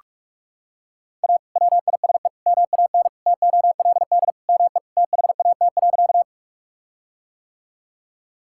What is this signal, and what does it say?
Morse code, heard as a keyed tone, about 44 wpm, 700 Hz. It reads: A WISE MAN TOLD ME THAT.
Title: Sentences from Top 1600 Words 45wpm